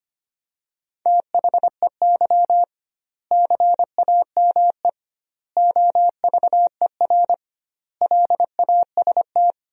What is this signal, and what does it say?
Morse code, heard as a keyed tone, about 25 wpm, 700 Hz. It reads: THEY CAME OVER LAST
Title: 2x Word-Spacing - Sentences from Top 300 Words - 25wpm